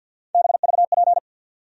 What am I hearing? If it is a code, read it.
Morse code, heard as a keyed tone, about 51 wpm, 700 Hz. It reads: BVP
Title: USA Call Sign Postfixes - 3 Letters 50wpm